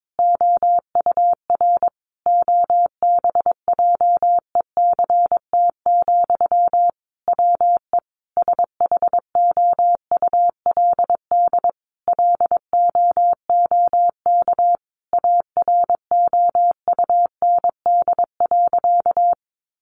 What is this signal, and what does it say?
Morse code, heard as a keyed tone, about 22 wpm, 700 Hz. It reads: OUR OBJECT, WE SHOULD LOOK AROUND.
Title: Sentences from Top 800 Words 22wpm